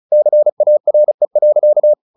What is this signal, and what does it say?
Morse code, heard as a keyed tone, about 34 wpm, 600 Hz. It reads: CARE.
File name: the old man and the sea 2 35wpm